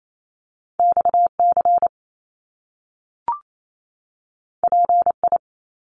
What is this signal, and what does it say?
Morse code, heard as a keyed tone, about 28 wpm, 700 Hz. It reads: XC PI
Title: Top Two Letter Combinations in Words 28wpm